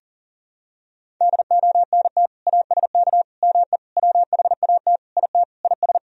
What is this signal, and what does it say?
Morse code, heard as a keyed tone, about 40 wpm, 700 Hz. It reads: DONT ASK ME WHAT IT IS
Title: Sentences from Top 1000 Words - Review Part 3 - 40wpm